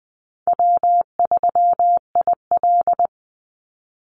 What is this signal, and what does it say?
Morse code, heard as a keyed tone, about 20 wpm, 700 Hz. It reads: W3IL
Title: Easy Call Signs Repeated 3x - 20wpm